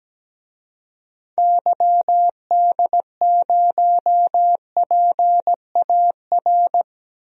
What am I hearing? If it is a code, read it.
Morse code, heard as a keyed tone, about 17 wpm, 700 Hz. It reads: YD0PAR
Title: DXCC-100 Call Signs - 6 Characters - Repeated 2x - 17wpm